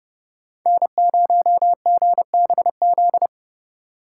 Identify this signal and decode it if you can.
Morse code, heard as a keyed tone, about 29 wpm, 700 Hz. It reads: N0GBZ